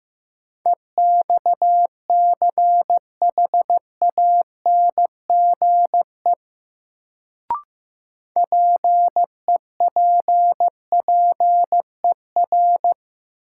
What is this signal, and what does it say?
Morse code, heard as a keyed tone, about 15 wpm, 700 Hz. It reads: EXCHANGE PEPPER
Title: Sets of 5 Words - New Word Drill - 15wpm